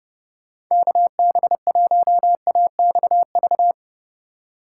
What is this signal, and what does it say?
Morse code, heard as a keyed tone, about 38 wpm, 700 Hz. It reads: KB1AXV